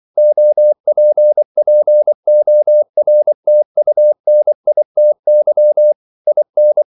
Morse code, heard as a keyed tone, about 24 wpm, 600 Hz. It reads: OPPORTUNITY IN
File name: CW / Practice / EnglishQuotations0424